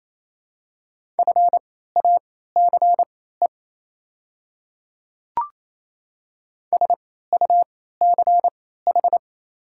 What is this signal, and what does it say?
Morse code, heard as a keyed tone, about 28 wpm, spaced as for 16 wpm, 690 Hz. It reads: FACE SUCH
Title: 3x Character Spacing - Top 300 Words - 28wpm